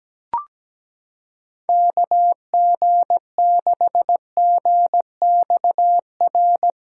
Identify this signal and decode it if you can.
Morse code, heard as a keyed tone, about 17 wpm, 700 Hz. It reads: KG6GXR
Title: USA Callsign DE Callsign 17wpm